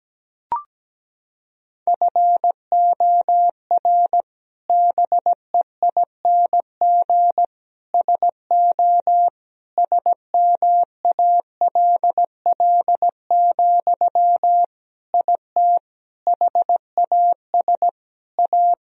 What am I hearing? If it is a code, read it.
Morse code, heard as a keyed tone, about 17 wpm, 700 Hz. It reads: FOR BEING SO SMALL, IT HAS A
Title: Sentences from Top 1000 Words - Review Part 3 - 17wpm